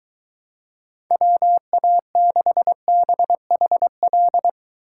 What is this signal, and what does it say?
Morse code, heard as a keyed tone, about 23 wpm, 700 Hz. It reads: WA6BHL